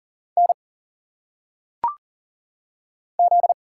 Morse code, heard as a keyed tone, about 40 wpm, 700 Hz. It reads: N Z
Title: Single Letters - Phonetic Pronunciation 40wpm